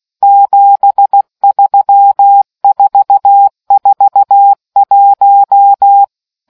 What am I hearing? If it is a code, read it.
Morse code, heard as a keyed tone, about 16 wpm, 800 Hz. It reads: 73441